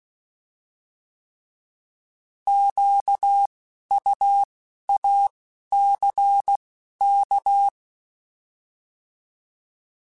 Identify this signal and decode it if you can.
Morse code, heard as a keyed tone, about 16 wpm, 800 Hz. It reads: Q U A C K
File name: Alphabetical / Alphabetical mp3s / Qs @10